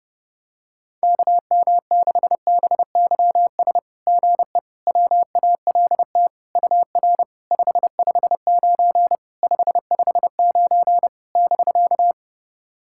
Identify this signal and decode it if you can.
Morse code, heard as a keyed tone, about 30 wpm, 690 Hz. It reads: KM6BYS GE WALT UR 559 559 <BK>
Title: SOTA QSO Practice 30wpm